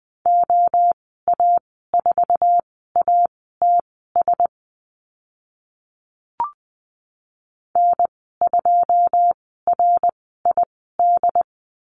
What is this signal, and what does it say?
Morse code, heard as a keyed tone, about 20 wpm, spaced as for 14 wpm, 700 Hz. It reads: OA4ATS N2RID
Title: 088 Lesson 15g - Number 5 - Call Signs 20wpm